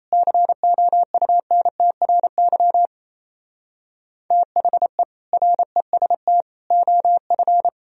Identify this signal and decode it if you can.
Morse code, heard as a keyed tone, about 37 wpm, 700 Hz. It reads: COUNTRY THE REST OF